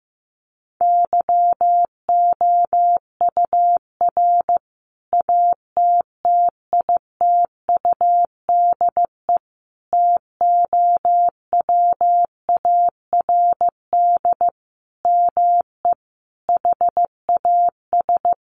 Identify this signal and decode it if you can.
Morse code, heard as a keyed tone, about 15 wpm, 700 Hz. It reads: YOUR ATTITUDE TOWARD ME HAS
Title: Sentences from Top 1300 Words 15wpm